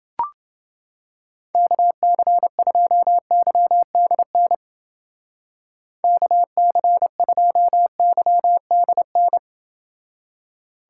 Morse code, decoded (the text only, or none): KC2YDN KC2YDN